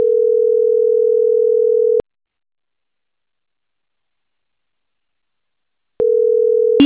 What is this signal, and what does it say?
Ringback tone (North America), 440 + 480 Hz.